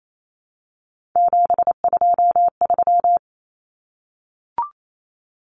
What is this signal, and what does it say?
Morse code, heard as a keyed tone, about 28 wpm, 700 Hz. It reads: 723
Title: Numbers - 3-Digit 28wpm